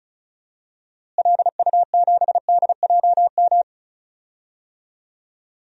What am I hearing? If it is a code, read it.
Morse code, heard as a keyed tone, about 36 wpm, 700 Hz. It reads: LU7DJM
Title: DXCC-100 Call Signs - 6 Characters - Repeated 2x - 35wpm